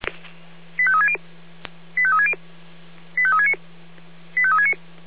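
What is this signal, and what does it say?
Five-tone selective call (ZVEI), heard as quick runs of five tones, tones of about 70 ms. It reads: 85360 85360 85369 85369
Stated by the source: ZVEI five-tone call received on an SDR.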